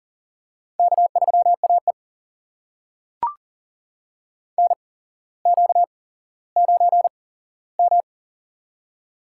Morse code, heard as a keyed tone, about 40 wpm, spaced as for 17 wpm, 700 Hz. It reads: K3AE NQ9M
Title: USA Call Signs - 8x Character Spacing - 40wpm